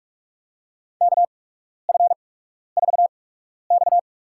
Morse code, heard as a keyed tone, about 44 wpm, spaced as for 12 wpm, 700 Hz. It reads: KF4X
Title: USA Call Signs - 4 Characters - 8x Character Spacing - 45wpm